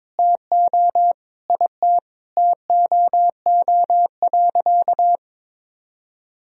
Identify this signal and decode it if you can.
Morse code, heard as a keyed tone, about 21 wpm, 700 Hz. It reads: TO IT TOO.
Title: Sentences from Top 1000 Words - Review Part 1 - 22wpm